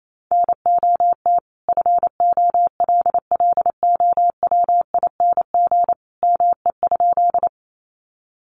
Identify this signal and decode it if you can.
Morse code, heard as a keyed tone, about 28 wpm, 700 Hz. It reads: NOT FOLLOWING ME?
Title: Sentences from Top 700 Words 28wpm